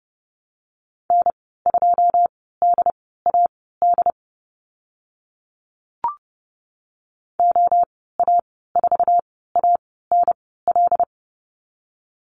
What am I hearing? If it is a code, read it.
Morse code, heard as a keyed tone, about 30 wpm, spaced as for 17 wpm, 700 Hz. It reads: N2DAD OA4ANL
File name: Lesson 14g - Number 2 - Call Signs 30wpm